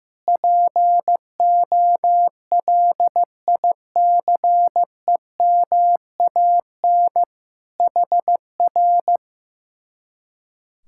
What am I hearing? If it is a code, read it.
Morse code, heard as a keyed tone, about 15 wpm, 700 Hz. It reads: POLICEMAN HR
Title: Speed-Racing Common QSO Phrases Encore - 15wpm